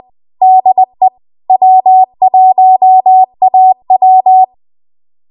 Morse code, heard as a keyed tone, about 20 wpm, 750 Hz. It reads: DE W1AW
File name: Archive / 20 WPM / 20WPM